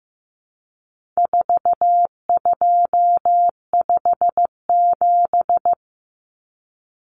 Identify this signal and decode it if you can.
Morse code, heard as a keyed tone, about 15 wpm, 700 Hz. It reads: 4257
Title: Numbers - 4-Digit 15wpm